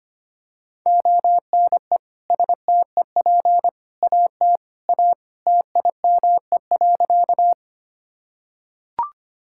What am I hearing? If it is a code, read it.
Morse code, heard as a keyed tone, about 25 wpm, 700 Hz. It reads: ONE STEP AT A TIME.
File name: practice / 25wpm / Sentences with 3-Word Phrases - 25wpm